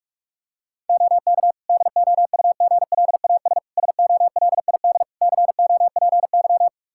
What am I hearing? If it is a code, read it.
Morse code, heard as a keyed tone, about 43 wpm, 700 Hz. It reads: OK DOUGLAS SOLID COPY
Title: Practice QSO - Exchange 3 - More Info - 45wpm